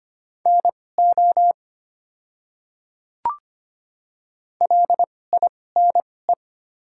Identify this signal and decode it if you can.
Morse code, heard as a keyed tone, about 25 wpm, spaced as for 18 wpm, 700 Hz. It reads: NO LINE
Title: 2x Character Spacing - Top 500 Words - 25wpm